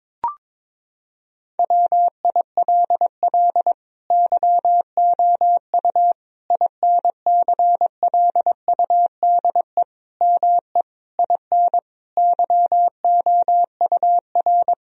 Morse code, heard as a keyed tone, about 22 wpm, 700 Hz. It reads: WILL YOU INCLUDE ME IN YOUR
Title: Sentences from Top 1000 Words - Review Part 3 - 22wpm